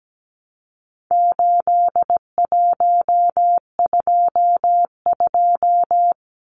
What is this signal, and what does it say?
Morse code, heard as a keyed tone, about 17 wpm, 700 Hz. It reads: 8122
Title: Numbers - 4-Digit 17wpm